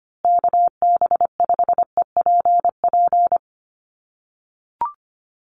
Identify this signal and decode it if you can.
Morse code, heard as a keyed tone, about 25 wpm, 700 Hz. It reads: KB5EPP
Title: USA Call Signs Repeated 2x 25wpm